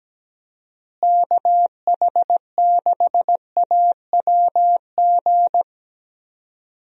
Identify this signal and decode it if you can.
Morse code, heard as a keyed tone, about 17 wpm, 700 Hz. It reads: KH6AWG
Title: DXCC-100 Call Signs - 6 Characters - Repeated 2x - 17wpm